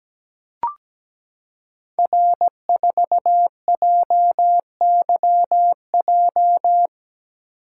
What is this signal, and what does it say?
Morse code, heard as a keyed tone, about 17 wpm, 690 Hz. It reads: R4JYJ
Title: Code Groups - 5 Letter:Numbers - 17wpm